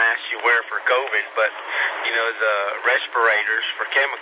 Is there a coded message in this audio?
Yes: Emergency Alert System (SAME) tones from 0.2 to 3.9 s, heard as harsh buzzing bursts, with speech.